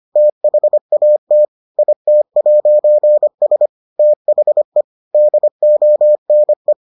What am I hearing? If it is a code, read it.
Morse code, heard as a keyed tone, about 25 wpm, 600 Hz. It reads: THAT IT'S THE DONE